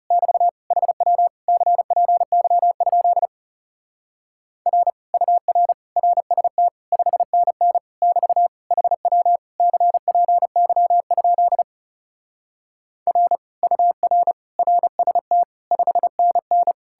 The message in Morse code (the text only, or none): <BT> HW CPY? R UR RST 5NN <BT> HW CPY? R UR RST 5NN